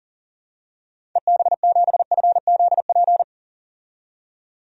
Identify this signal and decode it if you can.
Morse code, heard as a keyed tone, about 40 wpm, 700 Hz. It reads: EB7FZP
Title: DXCC-100 Call Signs - 6 Characters - 40wpm